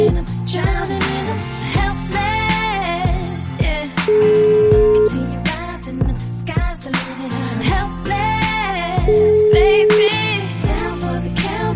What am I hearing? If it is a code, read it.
Ringback tone (Europe, 425 Hz), with noise.